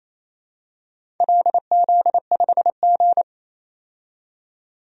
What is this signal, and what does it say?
Morse code, heard as a keyed tone, about 27 wpm, 700 Hz. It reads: LZ5G
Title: DXCC-100 Call Signs - 4 Characters - 28wpm